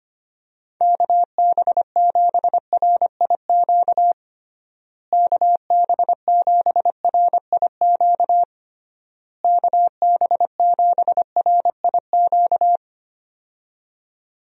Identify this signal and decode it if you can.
Morse code, heard as a keyed tone, about 25 wpm, 700 Hz. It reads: KB7RIQ KB7RIQ KB7RIQ
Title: USA Call Signs - 6 Characters - Repeated 3x - 25wpm